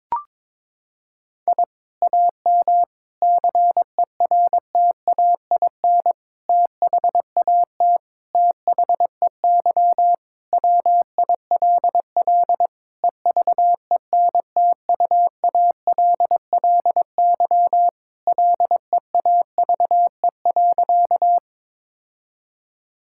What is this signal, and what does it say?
Morse code, heard as a keyed tone, about 22 wpm, 700 Hz. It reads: I AM CERTAIN THAT THEY WILL EVENTUALLY LEAVE.
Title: Sentences from Top 1000 Words - Review Part 6 - 22wpm